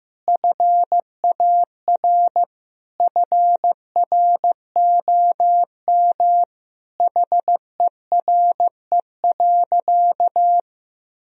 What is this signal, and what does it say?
Morse code, heard as a keyed tone, about 15 wpm, 700 Hz. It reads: FAR FROM HERE.
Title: Sentences from Top 1000 Words - Review Part 4 - 15wpm